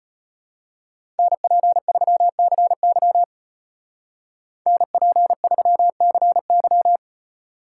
Morse code, heard as a keyed tone, about 33 wpm, 700 Hz. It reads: NP3CY NP3CY